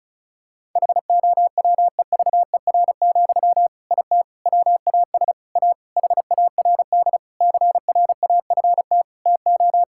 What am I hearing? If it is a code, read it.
Morse code, heard as a keyed tone, about 36 wpm, 700 Hz. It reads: HOWEVER, IT WAS A HARD CRAFT TO
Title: Sentences from Top 1000 Words - Review Part 6 - 35wpm